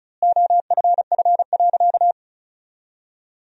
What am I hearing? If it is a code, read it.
Morse code, heard as a keyed tone, about 36 wpm, 700 Hz. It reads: OFF.
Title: 2x Word-Spacing - Sentences from Top 500 Words - 35wpm